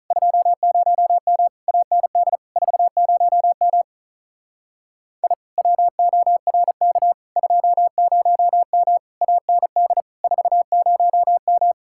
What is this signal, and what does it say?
Morse code, heard as a keyed tone, about 37 wpm, 700 Hz. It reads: 20M AND 40M I WORK 20M AND 40M
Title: Speed-Racing Common QSO Phrases - 35wpm